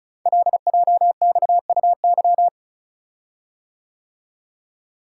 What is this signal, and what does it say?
Morse code, heard as a keyed tone, about 36 wpm, 700 Hz. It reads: LJXUY
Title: Code Groups - 5 Letter:Numbers - 35wpm